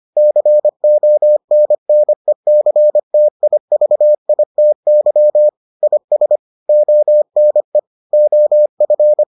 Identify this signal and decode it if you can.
Morse code, heard as a keyed tone, about 25 wpm, 600 Hz. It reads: CONNECTIVITY IS ONE OF